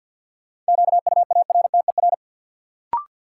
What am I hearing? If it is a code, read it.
Morse code, heard as a keyed tone, about 53 wpm, 700 Hz. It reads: QUARTER
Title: Sentences from Top 1300 Words - New Word Drill - 50wpm